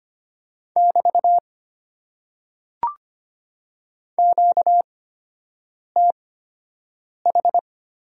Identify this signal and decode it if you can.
Morse code, heard as a keyed tone, about 25 wpm, spaced as for 7 wpm, 700 Hz. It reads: <BT> QTH
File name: Top 90 QSO Elements - 8x Character Spacing - 25wpm